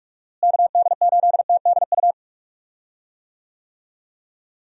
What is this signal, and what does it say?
Morse code, heard as a keyed tone, about 46 wpm, 700 Hz. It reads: KD8TDU
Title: USA Call Signs - 6 Characters - Repeated 3x - 45wpm